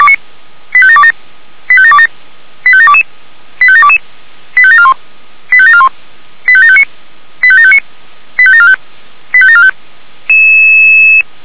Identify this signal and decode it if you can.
Five-tone selective call (ZVEI), heard as quick runs of five tones, tones of about 70 ms. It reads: 85727 85727 85722 85722 85741 85741 85759 85759 85745 85745
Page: SDR/ZVEI